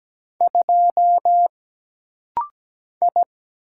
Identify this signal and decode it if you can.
Morse code, heard as a keyed tone, about 17 wpm, 700 Hz. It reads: 2 I
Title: Single Letter-Number Alt - Rapid-Fire 17wpm